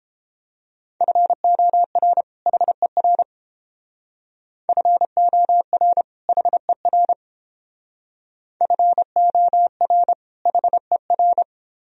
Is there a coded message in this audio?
Yes: Morse code from 1.0 to 11.4 s, heard as a keyed tone, about 30 wpm, 700 Hz. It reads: FOR HER FOR HER FOR HER